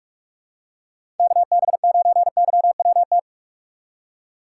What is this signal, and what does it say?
Morse code, heard as a keyed tone, about 46 wpm, 700 Hz. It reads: KB9YWT
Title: USA Call Signs - 6 Characters - 2x Character Spacing - 45wpm